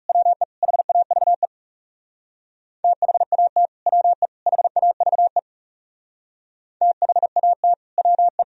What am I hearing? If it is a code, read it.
Morse code, heard as a keyed tone, about 36 wpm, 700 Hz. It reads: WE HAVE THAT WE HAVE THAT WE
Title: Speed-Racing - Sets of 3 Words 30wpm